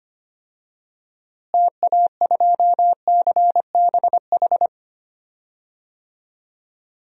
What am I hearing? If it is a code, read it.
Morse code, heard as a keyed tone, about 25 wpm, 700 Hz. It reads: TA2CBH
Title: DXCC-100 Call Signs - 6 Characters - Repeated 2x - 25wpm